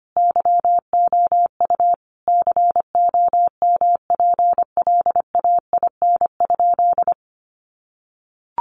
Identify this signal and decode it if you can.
Morse code, heard as a keyed tone, about 25 wpm, 700 Hz. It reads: YOU COMPLAIN?
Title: Sentences from Top 1300 Words 25wpm